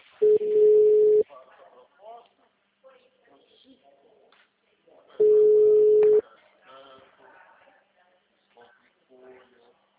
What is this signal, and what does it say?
Ringback tone (Europe, 425 Hz), with speech.
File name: out- -2010- - - .42785